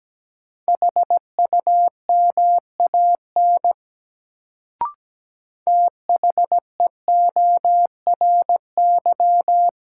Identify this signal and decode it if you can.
Morse code, heard as a keyed tone, about 17 wpm, 700 Hz. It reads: HUMAN THEORY
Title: Sentences from Top 600 Words - New Word Drill - 17wpm